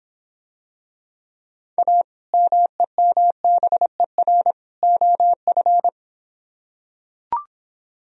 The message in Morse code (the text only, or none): A MEMBER OF